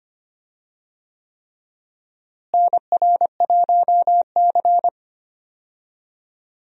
Morse code, heard as a keyed tone, about 25 wpm, 700 Hz. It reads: NR1C
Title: USA Call Signs - 4 Characters - Repeated 3x - 25wpm